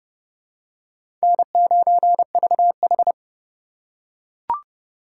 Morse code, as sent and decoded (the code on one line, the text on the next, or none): -. ----. ...- ....
N9VH